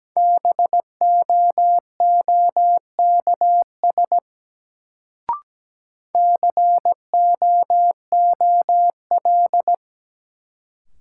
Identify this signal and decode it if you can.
Morse code, heard as a keyed tone, about 17 wpm, 700 Hz. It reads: BOOKS COOL
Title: Sentences from Top 1200 Words - New Word Drill Alt - 17wpm